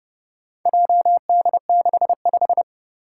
Morse code, heard as a keyed tone, about 30 wpm, 700 Hz. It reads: JD65
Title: Four Challenging Letter-Number Combinations 30wpm